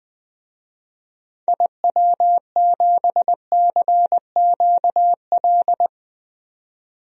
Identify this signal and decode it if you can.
Morse code, heard as a keyed tone, about 21 wpm, 700 Hz. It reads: IW7CQL
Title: DXCC-100 Call Signs - 6 Characters - 20wpm